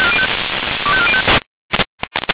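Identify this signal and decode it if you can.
Five-tone selective call (ZVEI), heard as quick runs of five tones, tones of about 70 ms. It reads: 25405
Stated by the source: ZVEI five-tone selective call received by SDR.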